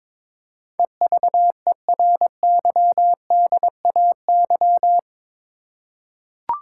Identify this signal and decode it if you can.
Morse code, heard as a keyed tone, about 22 wpm, 700 Hz. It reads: EVERYDAY
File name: practice / 22wpm / Eight Letter Words - 22wpm